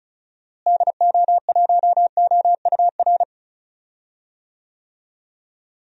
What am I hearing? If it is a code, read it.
Morse code, heard as a keyed tone, about 34 wpm, 700 Hz. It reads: DO1OUR
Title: DXCC-100 Call Signs - 6 Characters - Repeated 2x - 35wpm